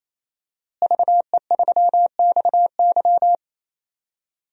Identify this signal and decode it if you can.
Morse code, heard as a keyed tone, about 28 wpm, 690 Hz. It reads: VE3XY
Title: DXCC-100 Call Signs - 5 Characters - Repeated 3x - 28wpm